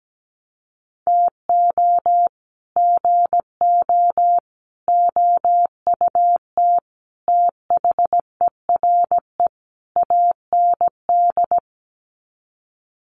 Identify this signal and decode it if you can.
Morse code, heard as a keyed tone, about 17 wpm, 700 Hz. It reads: TO GO OUT THERE AND
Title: Sets of 5 Words Encore2 17wpm